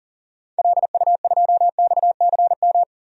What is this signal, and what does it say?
Morse code, heard as a keyed tone, about 40 wpm, 700 Hz. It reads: LU2XCM
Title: DXCC-100 Call Signs - 6 Characters - 40wpm